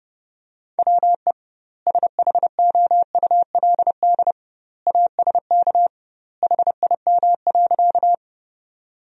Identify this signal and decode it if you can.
Morse code, heard as a keyed tone, about 32 wpm, 700 Hz. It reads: WE SHOULD ASK HIM.
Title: 2x Word-Spacing - Sentences from Top 300 Words - 30wpm